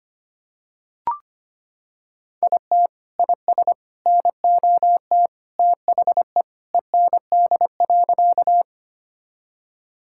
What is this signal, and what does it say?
Morse code, heard as a keyed tone, about 25 wpm, 700 Hz. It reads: IT IS NOT THE END.
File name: practice / 25wpm / Sentences with 3-Word Phrases - 25wpm